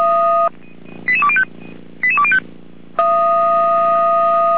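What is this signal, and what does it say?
Five-tone selective call (ZVEI), heard as quick runs of five tones, tones of about 70 ms. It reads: 88205 88205
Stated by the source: ZVEI five-tone selective call 88199 received on SDR, from the feuerwehr-orlamuende.org fire brigade audio archive.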